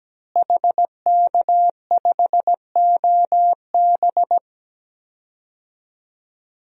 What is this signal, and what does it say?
Morse code, heard as a keyed tone, about 17 wpm, 700 Hz. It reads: HK5OB
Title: DXCC-100 Call Signs - 5 Characters - Repeated 2x - 17wpm